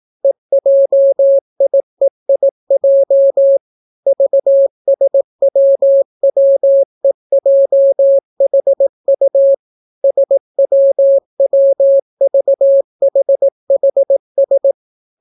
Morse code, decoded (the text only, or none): EJIEIJ VSWWEJHU SWWVHHS